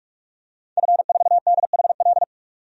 Morse code, heard as a keyed tone, about 46 wpm, 700 Hz. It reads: F4DHL
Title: DXCC-100 Call Signs - 5 Characters - 45wpm